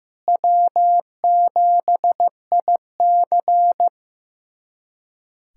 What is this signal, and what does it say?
Morse code, heard as a keyed tone, about 16 wpm, 700 Hz. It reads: W7IC
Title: USA Callsign DE Callsign - 4 Characters - 15wpm